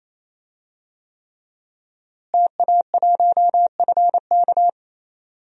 Morse code, heard as a keyed tone, about 28 wpm, 700 Hz. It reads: TA1FK
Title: DXCC-100 Call Signs - 5 Characters - Repeated 3x - 28wpm